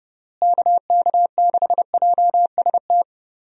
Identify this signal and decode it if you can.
Morse code, heard as a keyed tone, about 32 wpm, 700 Hz. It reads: KK6JST
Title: USA Call Signs - 6 Characters - Repeated 2x - 30wpm